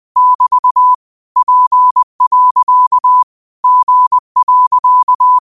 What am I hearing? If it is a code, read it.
Morse code, heard as a keyed tone, about 20 wpm, 1000 Hz. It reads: <BT> P. G.